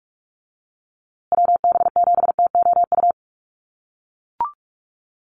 Morse code, heard as a keyed tone, about 44 wpm, 700 Hz. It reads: WB7TOU
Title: USA Call Signs Repeated 2x 45wpm